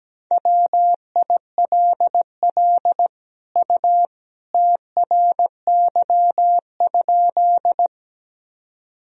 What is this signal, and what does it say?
Morse code, heard as a keyed tone, about 17 wpm, 700 Hz. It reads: WILL U TRY?
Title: Sentences from Top 200 Words and Abbreviations 17wpm